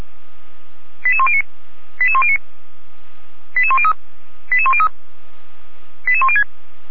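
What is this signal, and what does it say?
Five-tone selective call (ZVEI), heard as quick runs of five tones, tones of about 70 ms. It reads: 88109 88109 88103 88103 88106
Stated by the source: Received on an SDR, ZVEI five-tone selective call.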